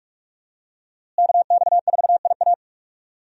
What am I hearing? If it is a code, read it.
Morse code, heard as a keyed tone, about 42 wpm, 700 Hz. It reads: KX4IA
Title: USA Call Signs - 5 Characters - Repeated 3x - 45wpm